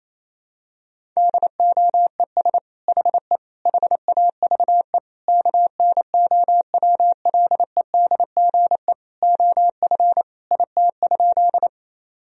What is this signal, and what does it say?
Morse code, heard as a keyed tone, about 28 wpm, 700 Hz. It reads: DOES HE HAVE KNOWLEDGE OF IT?
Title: Sentences from Top 1000 Words - Review Part 3 - 28wpm